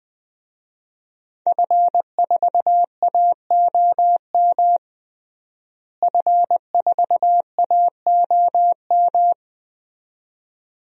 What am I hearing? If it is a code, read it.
Morse code, heard as a keyed tone, about 21 wpm, 700 Hz. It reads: F4AOM F4AOM
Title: DXCC-100 Call Signs - 5 Characters - Repeated 2x - 20wpm